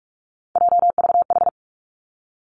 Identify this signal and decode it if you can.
Morse code, heard as a keyed tone, about 46 wpm, 700 Hz. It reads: JVH
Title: Three Challenging Letter-Number Combinations 45wpm